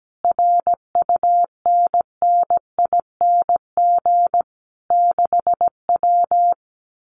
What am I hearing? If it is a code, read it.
Morse code, heard as a keyed tone, about 17 wpm, 700 Hz. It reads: RUNNING 6W
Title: Common QSO Phrases Encore 17wpm